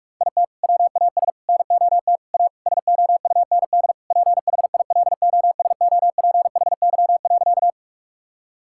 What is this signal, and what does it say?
Morse code, heard as a keyed tone, about 41 wpm, 700 Hz. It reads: IT WAS NOT A SOUND PHILOSOPHY.
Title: Sentences from Top 1000 Words - Review Part 6 - 45wpm